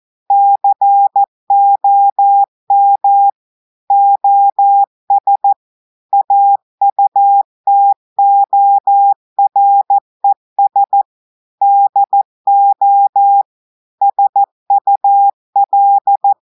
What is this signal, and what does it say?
Morse code, heard as a keyed tone, about 14 wpm, 800 Hz. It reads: COM OS AUTORES DO SUL
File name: audios / portugais / portugais texte en clair 1 vitesse 14 mots minutes